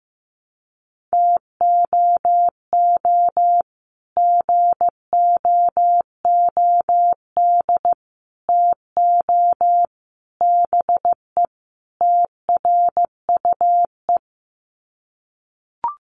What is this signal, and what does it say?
Morse code, heard as a keyed tone, about 15 wpm, 700 Hz. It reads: TOO GOOD TO BE TRUE
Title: Sets of 5 Words Encore 15wpm